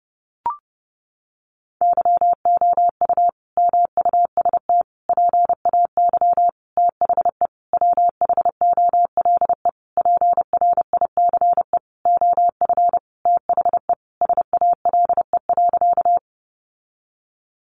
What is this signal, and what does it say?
Morse code, heard as a keyed tone, about 30 wpm, 700 Hz. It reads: YOU MUST PAY THE WHOLE PRICE OF THE SALE.
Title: Sentences from Top 1400 Words 30wpm